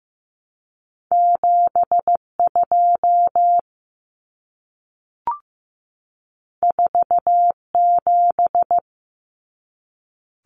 Morse code, heard as a keyed tone, about 15 wpm, 700 Hz. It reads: 72 47
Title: Common Cut Numbers - 2-Digit 15wpm